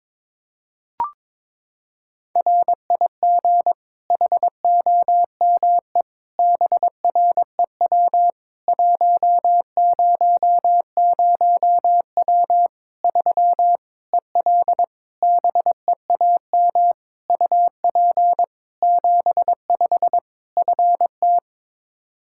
Morse code, heard as a keyed tone, about 22 wpm, 700 Hz. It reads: RIG HOME BREW 100W 3 EL BEAM UP 75 FT
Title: K1USN SST QSO Practice 22wpm